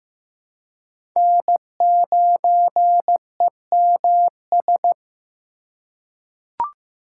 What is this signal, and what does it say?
Morse code, heard as a keyed tone, about 15 wpm, 700 Hz. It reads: N9EMS
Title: USA Call Signs - 3x Character Spacing - 15wpm